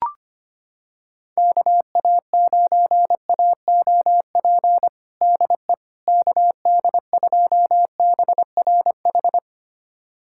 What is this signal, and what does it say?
Morse code, heard as a keyed tone, about 25 wpm, 700 Hz. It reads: KA9AOP DE KD2BRH
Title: USA Callsign DE Callsign - 6 Characters - 25wpm